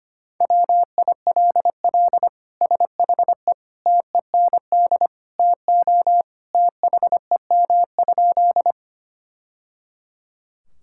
Morse code, heard as a keyed tone, about 25 wpm, 700 Hz. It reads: WILL SHE TEND TO THEM?
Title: Sentences from Top 1000 Words - Review Part 5 - 25wpm